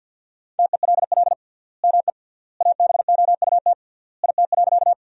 Morse code, heard as a keyed tone, about 50 wpm, 700 Hz. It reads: TELL ME ABOUT IT.
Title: 3x Word-Spacing - Sentences from Top 200 Words - 50wpm